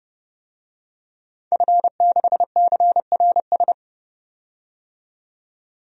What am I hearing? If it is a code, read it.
Morse code, heard as a keyed tone, about 32 wpm, 700 Hz. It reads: F6CRS
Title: DXCC-100 Call Signs - 5 Characters - 30wpm